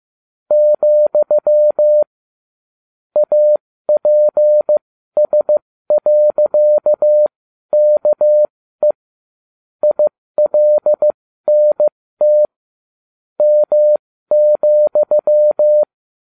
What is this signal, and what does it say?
Morse code, heard as a keyed tone, about 15 wpm, spaced as for 13 wpm, 600 Hz. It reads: , APS.KE ILNT M,